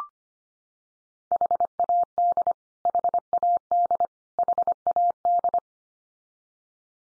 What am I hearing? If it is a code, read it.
Morse code, heard as a keyed tone, about 25 wpm, 700 Hz. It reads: HAD HAD HAD
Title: Top 100 Words 25wpm x3